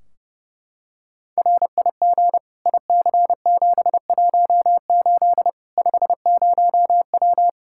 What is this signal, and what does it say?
Morse code, heard as a keyed tone, about 30 wpm, 700 Hz. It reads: RIG IC718 50W
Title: Practice QSO - Exchange 2 - Info - 30wpm